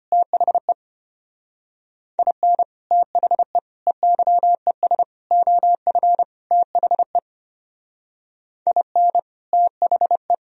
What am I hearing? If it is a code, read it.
Morse code, heard as a keyed tone, about 29 wpm, 700 Hz. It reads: THE IN THE EYES OF THE IN THE
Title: Speed-Racing - Sets of 5 Words 25wpm